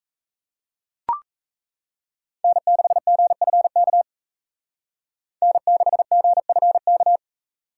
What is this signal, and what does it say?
Morse code, heard as a keyed tone, about 37 wpm, 700 Hz. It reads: N6GFK N6GFK